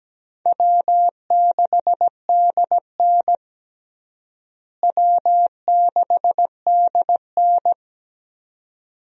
Morse code, heard as a keyed tone, about 17 wpm, 700 Hz. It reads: W6DN W6DN
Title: USA Call Signs - 4 Characters - Repeated 2x - 17wpm